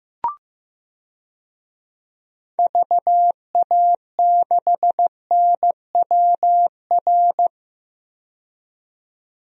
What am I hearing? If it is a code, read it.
Morse code, heard as a keyed tone, about 15 wpm, 700 Hz. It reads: VA6NWR
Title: DXCC-100 Call Signs - 6 Characters - 15wpm